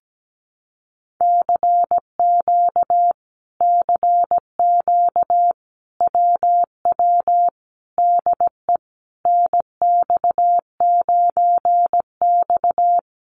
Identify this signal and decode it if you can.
Morse code, heard as a keyed tone, about 17 wpm, 700 Hz. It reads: CQ CQ WW DE NX9X
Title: Common QSO Phrases Encore 17wpm